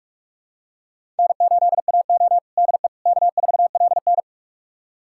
Morse code, heard as a keyed tone, about 47 wpm, 700 Hz. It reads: N8AO DE K4LN
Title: USA Callsign DE Callsign - 4 Characters - 45wpm